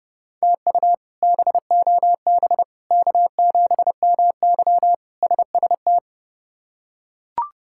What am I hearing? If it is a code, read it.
Morse code, heard as a keyed tone, about 30 wpm, 700 Hz. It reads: TU BOB K7MY SST
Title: K1USN SST QSO Practice - 30wpm